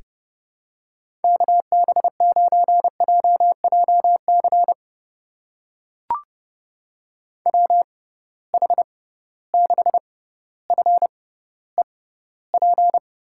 Morse code, heard as a keyed tone, about 28 wpm, spaced as for 15 wpm, 700 Hz. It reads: KB9JJC WH6FEP
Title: USA Call Signs - 6x Character Spacing - 30wpm